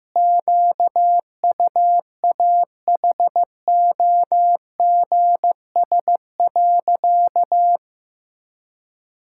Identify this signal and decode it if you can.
Morse code, heard as a keyed tone, about 15 wpm, 700 Hz. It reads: QUAHOGS.
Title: Pangram Challenge v2 - 15wpm